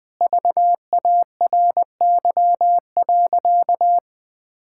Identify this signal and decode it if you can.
Morse code, heard as a keyed tone, about 21 wpm, 700 Hz. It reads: VARY.
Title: Sentences with 6-Word Phrases - 20wpm